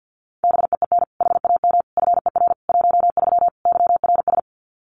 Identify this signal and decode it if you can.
Morse code, heard as a keyed tone, about 50 wpm, 700 Hz. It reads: BEEN HAM FER 13 YRS
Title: Common QSO Phrases 50wpm